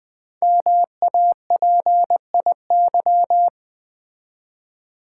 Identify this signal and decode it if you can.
Morse code, heard as a keyed tone, about 20 wpm, 700 Hz. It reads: MAPIY
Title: Code Groups - 5 Letters - 20wpm